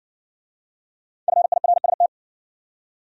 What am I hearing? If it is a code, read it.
Morse code, heard as a keyed tone, about 60 wpm, 700 Hz. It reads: FIRST